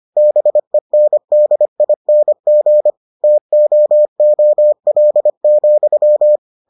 Morse code, heard as a keyed tone, about 25 wpm, 600 Hz. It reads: BENDING TOOL,